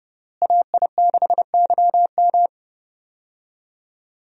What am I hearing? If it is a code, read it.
Morse code, heard as a keyed tone, about 30 wpm, 700 Hz. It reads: AI6YM
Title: USA Call Signs - 5 Characters - Repeated 3x - 30wpm